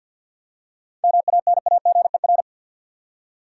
Morse code, heard as a keyed tone, about 51 wpm, 700 Hz. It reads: MANAGER
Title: Seven Letter Words - 50wpm